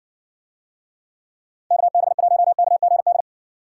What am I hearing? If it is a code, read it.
Morse code, heard as a keyed tone, about 64 wpm, 700 Hz. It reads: KB1KGD